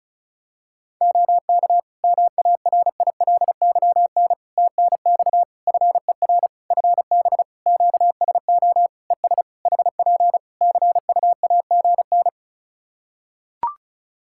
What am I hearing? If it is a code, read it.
Morse code, heard as a keyed tone, about 34 wpm, 700 Hz. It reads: OK MARILYN TNX FER FB QSO ES HP CUAGN
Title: Practice QSO - Exchange 4 - Ending - 35wpm